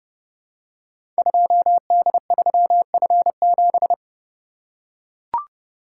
Morse code, heard as a keyed tone, about 30 wpm, 700 Hz. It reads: 2D3F7
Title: Code Groups - 5 Letter:Numbers - 30wpm